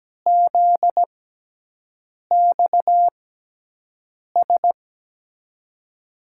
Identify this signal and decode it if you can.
Morse code, heard as a keyed tone, about 17 wpm, spaced as for 6 wpm, 700 Hz. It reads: ZXS
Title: USA Call Signs - 6 Characters - 6x Character Spacing - 17wpm